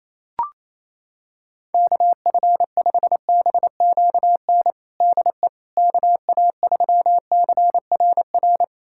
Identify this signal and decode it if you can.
Morse code, heard as a keyed tone, about 28 wpm, 700 Hz. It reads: KF5BQN DE KA3CRR
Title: USA Callsign DE Callsign - 6 Characters - 28wpm